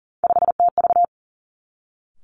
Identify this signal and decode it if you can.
Morse code, heard as a keyed tone, about 39 wpm, 700 Hz. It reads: 5TV
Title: Three Challenging Letter-Number Combinations 40wpm